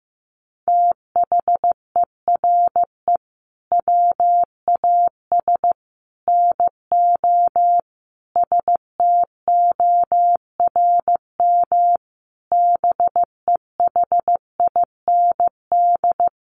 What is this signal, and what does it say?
Morse code, heard as a keyed tone, about 14 wpm, 700 Hz. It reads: THERE WAS NO STORM BEHIND
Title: Sentences from Top 1500 Words 15wpm